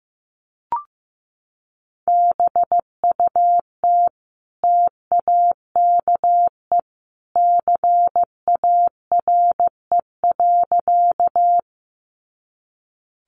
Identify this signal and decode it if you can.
Morse code, heard as a keyed tone, about 15 wpm, 700 Hz. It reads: BUT TAKE CARE.
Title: Sentences from Top 300 Words 15wpm